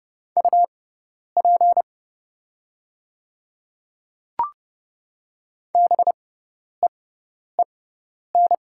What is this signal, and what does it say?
Morse code, heard as a keyed tone, about 30 wpm, spaced as for 10 wpm, 690 Hz. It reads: UP BEEN
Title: 6x Character Spacing - Top 200 Words - 30wpm